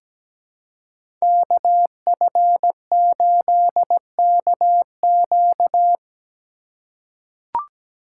Morse code, heard as a keyed tone, about 17 wpm, 700 Hz. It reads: KF8KQ
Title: USA Call Signs - 4x Character Spacing - 17wpm